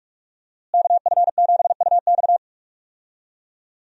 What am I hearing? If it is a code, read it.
Morse code, heard as a keyed tone, about 46 wpm, 700 Hz. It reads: KF7UX
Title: USA Call Signs - 5 Characters - Repeated 3x - 45wpm